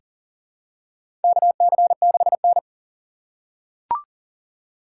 Morse code, heard as a keyed tone, about 40 wpm, 700 Hz. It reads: KC6N
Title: USA Call Signs - 6x Character Spacing - 40wpm